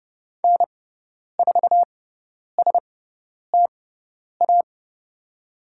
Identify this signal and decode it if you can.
Morse code, heard as a keyed tone, about 30 wpm, spaced as for 10 wpm, 700 Hz. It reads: N4STA
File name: Lesson 16g - Letter c - Call Signs 30wpm